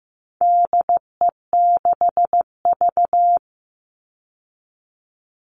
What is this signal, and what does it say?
Morse code, heard as a keyed tone, about 15 wpm, 700 Hz. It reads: DE6V
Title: All Cut Numbers - 4-Digit 15wpm